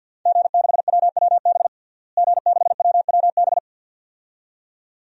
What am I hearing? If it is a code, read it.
Morse code, heard as a keyed tone, about 51 wpm, 700 Hz. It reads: G6WWB G6WWB
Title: DXCC-100 Call Signs - 5 Characters - Repeated 2x - 50wpm